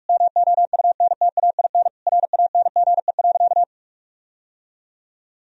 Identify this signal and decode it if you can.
Morse code, heard as a keyed tone, about 46 wpm, 700 Hz. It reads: MOUNTAIN RANGE.
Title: Sentences from Top 1000 Words - Review Part 4 - 45wpm